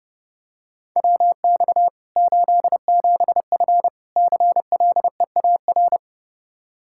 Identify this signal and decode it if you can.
Morse code, heard as a keyed tone, about 30 wpm, 700 Hz. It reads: WX 87F CLEAR
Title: Practice QSO - Exchange 2 - Info - 30wpm